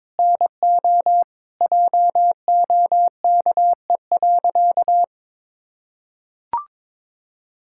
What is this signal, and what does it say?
Morse code, heard as a keyed tone, about 22 wpm, 700 Hz. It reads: NO JOKE.
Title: Pangram Challenge v2 - 22wpm